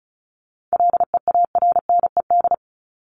Morse code, heard as a keyed tone, about 34 wpm, 700 Hz. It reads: LEARNED
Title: ED Suffix - Top 100 Words with Base Words 35wpm